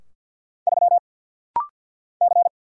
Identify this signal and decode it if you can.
Morse code, heard as a keyed tone, about 48 wpm, 700 Hz. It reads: 3 C
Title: Single Letter-Number - Rapid-Fire 50wpm